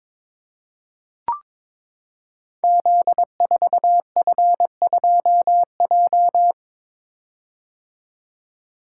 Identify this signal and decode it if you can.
Morse code, heard as a keyed tone, about 22 wpm, 700 Hz. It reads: Z4F2J
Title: Code Groups - 5 Letter:Numbers - 22wpm